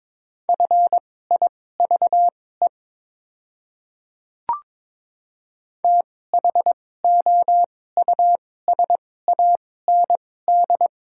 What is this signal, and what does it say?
Morse code, heard as a keyed tone, about 21 wpm, spaced as for 16 wpm, 700 Hz. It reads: FIVE THOUSAND
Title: 2x Character Spacing - Top 500 Words - 22wpm